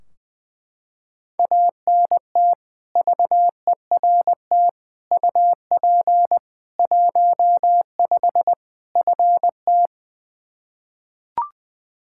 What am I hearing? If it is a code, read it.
Morse code, heard as a keyed tone, about 20 wpm, 700 Hz. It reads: ANT VERT UP 15 FT
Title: Practice QSO - Exchange 2 - Info - 20wpm